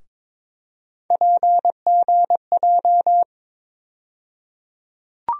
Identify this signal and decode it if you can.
Morse code, heard as a keyed tone, about 22 wpm, 700 Hz. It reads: PGJ
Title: USA Call Sign Postfixes - 3 Letters 22wpm